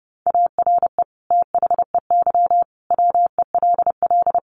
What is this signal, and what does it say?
Morse code, heard as a keyed tone, about 30 wpm, 700 Hz. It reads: ARE THEY WELL
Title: Sentences from Top 1300 Words 30wpm